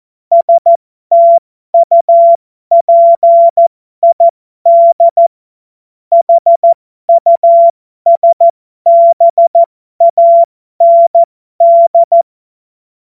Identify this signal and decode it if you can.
Morse code, heard as a keyed tone, about 14 wpm, spaced as for 12 wpm, 680 Hz. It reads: STUPID HUSBAND